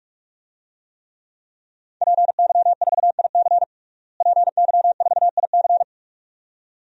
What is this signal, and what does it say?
Morse code, heard as a keyed tone, about 46 wpm, 700 Hz. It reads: PY4IC PY4IC
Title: DXCC-100 Call Signs - 5 Characters - Repeated 2x - 45wpm